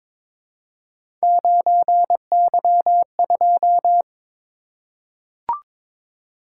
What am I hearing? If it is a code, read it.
Morse code, heard as a keyed tone, about 22 wpm, 700 Hz. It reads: 9Y2
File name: practice / 22wpm / Three Letter-Number Combinations - 22wpm